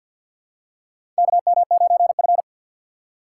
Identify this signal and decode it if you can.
Morse code, heard as a keyed tone, about 51 wpm, 700 Hz. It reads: KM9F
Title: USA Call Signs - 4 Characters - 6x Character Spacing - 50wpm